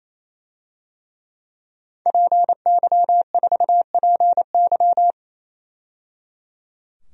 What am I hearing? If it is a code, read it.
Morse code, heard as a keyed tone, about 28 wpm, 700 Hz. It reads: PY4PY
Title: DXCC-100 Call Signs - 5 Characters - 28wpm